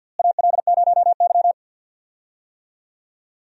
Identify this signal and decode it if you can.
Morse code, heard as a keyed tone, about 53 wpm, 700 Hz. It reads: AL0Y
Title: USA Call Signs - 4 Characters - Repeated 3x - 50wpm